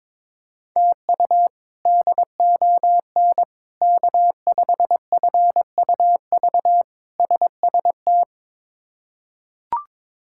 Morse code, heard as a keyed tone, about 22 wpm, 700 Hz. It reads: TU DON K5FUV SST
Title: K1USN SST QSO Practice - 22wpm